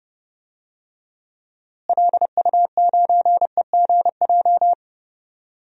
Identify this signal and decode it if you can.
Morse code, heard as a keyed tone, about 30 wpm, 700 Hz. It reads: LU9EGJ
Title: DXCC-100 Call Signs - 6 Characters - Repeated 2x - 30wpm